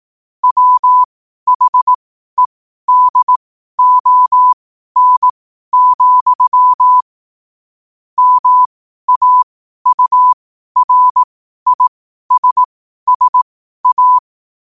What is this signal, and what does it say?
Morse code, heard as a keyed tone, about 18 wpm, spaced as for 12 wpm, 1000 Hz. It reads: WHEDON, MAURISSA
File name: qotdfiles / 12wpm